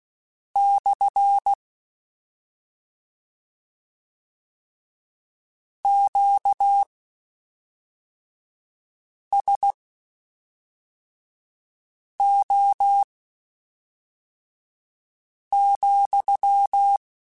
Morse code, heard as a keyed tone, about 16 wpm, spaced as for 3 wpm, 800 Hz. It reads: / QSO,
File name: Q Calls / Q Calls-mp3s / Q Calls SPSL2 @5